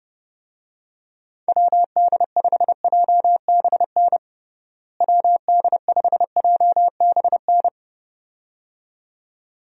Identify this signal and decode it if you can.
Morse code, heard as a keyed tone, about 30 wpm, 700 Hz. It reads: WD5JBN WD5JBN
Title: USA Call Signs - 6 Characters - Repeated 2x - 30wpm